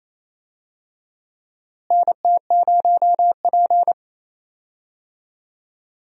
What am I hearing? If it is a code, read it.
Morse code, heard as a keyed tone, about 28 wpm, 700 Hz. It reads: NT0P
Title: USA Call Signs - 4 Characters - Repeated 3x - 28wpm